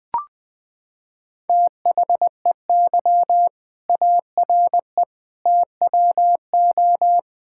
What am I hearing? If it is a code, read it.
Morse code, heard as a keyed tone, about 20 wpm, 700 Hz. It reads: THEY ARE TWO
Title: Sentences with 7-Word Phrases - 20wpm